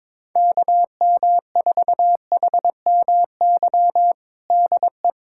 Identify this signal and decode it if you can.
Morse code, heard as a keyed tone, about 22 wpm, 700 Hz. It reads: KM4HMY DE
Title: USA Callsign DE Callsign - 6 Characters - 22wpm